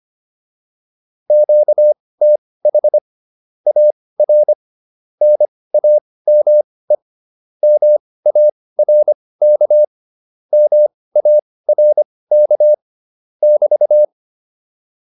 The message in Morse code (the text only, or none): QTH AR NAME MARK MARK <BT>